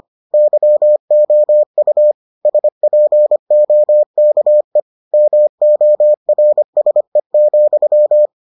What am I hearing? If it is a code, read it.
Morse code, heard as a keyed tone, about 25 wpm, 600 Hz. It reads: YOU SPOKE MORSE,